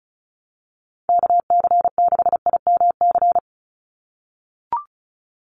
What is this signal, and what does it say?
Morse code, heard as a keyed tone, about 34 wpm, 700 Hz. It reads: KC6IMC
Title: USA Call Signs 35wpm